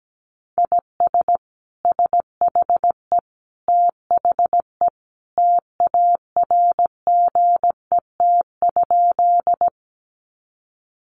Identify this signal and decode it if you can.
Morse code, heard as a keyed tone, about 17 wpm, 700 Hz. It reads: IS SHE THE TARGET?
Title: Sentences from Top 1200 Words 17wpm